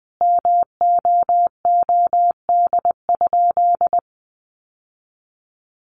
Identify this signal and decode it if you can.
Morse code, heard as a keyed tone, about 20 wpm, 700 Hz. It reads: MOOD?
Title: Sentences from Top 1100 Words 20wpm